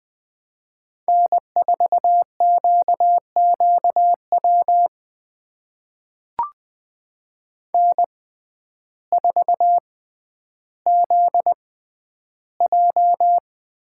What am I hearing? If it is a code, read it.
Morse code, heard as a keyed tone, about 20 wpm, spaced as for 11 wpm, 700 Hz. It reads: N4QQW N4ZJ
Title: USA Call Signs - 6x Character Spacing - 20wpm